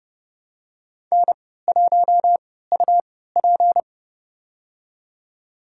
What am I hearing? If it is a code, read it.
Morse code, heard as a keyed tone, about 31 wpm, spaced as for 17 wpm, 700 Hz. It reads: N1UP
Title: USA Call Signs - 4 Characters - 3x Character Spacing - 30wpm